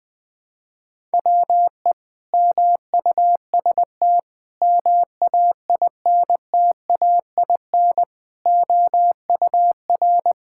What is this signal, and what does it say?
Morse code, heard as a keyed tone, about 20 wpm, 700 Hz. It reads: WE MUST MAINTAIN OUR
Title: Sentences from Top 1000 Words - Review Part 5 - 20wpm